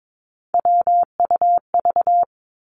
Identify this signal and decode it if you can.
Morse code, heard as a keyed tone, about 22 wpm, 700 Hz. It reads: WUV
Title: Three Challenging Letter-Number Combinations 22wpm